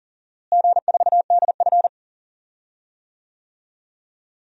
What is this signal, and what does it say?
Morse code, heard as a keyed tone, about 40 wpm, 700 Hz. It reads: G4DF
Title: DXCC-100 Call Signs - 4 Characters - Repeated 3x - 40wpm